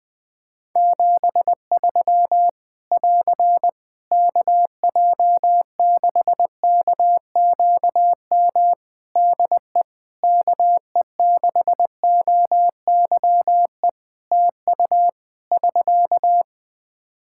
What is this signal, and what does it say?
Morse code, heard as a keyed tone, about 20 wpm, 700 Hz. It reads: 73 <AR> KJ6KQM DE KE6OYE TU <SK>
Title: Practice QSO - Exchange 4 - Ending - 20wpm